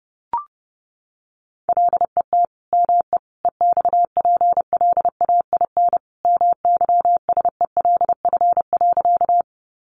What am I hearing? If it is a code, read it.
Morse code, heard as a keyed tone, about 28 wpm, 700 Hz. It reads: LET ME EXPLAIN MYSELF.
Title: Sentences from Top 1300 Words 30wpm